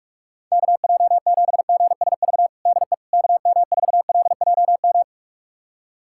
Morse code, heard as a keyed tone, about 46 wpm, 700 Hz. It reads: KJ7GSV DE KM4LJM
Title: USA Callsign DE Callsign - 6 Characters - 45wpm